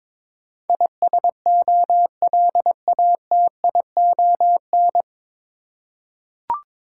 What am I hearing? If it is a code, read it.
Morse code, heard as a keyed tone, about 22 wpm, 700 Hz. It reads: ISOLATION
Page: Morse Code Ninja practice 22wpm